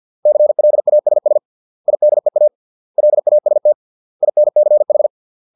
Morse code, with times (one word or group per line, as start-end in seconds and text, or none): 0.3-1.4 CLASS
1.9-2.5 IDEA
3.0-3.7 LAST
4.2-5.1 INCH